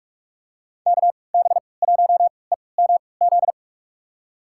Morse code, heard as a keyed tone, about 42 wpm, spaced as for 25 wpm, 700 Hz. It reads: KB1EMZ
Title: USA Call Signs - 6 Characters - 3x Character Spacing - 45wpm